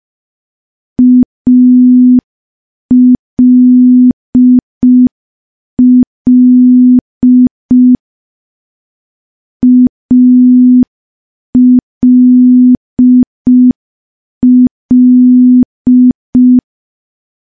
Morse code, heard as a keyed tone, about 5 wpm, 250 Hz. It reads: ALL ALL
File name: morse / morse-training-05WPM